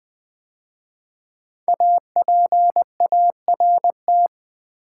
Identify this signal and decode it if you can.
Morse code, heard as a keyed tone, about 20 wpm, 700 Hz. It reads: APART
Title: 5 Letter Words - First 3 Letters Given - 20wpm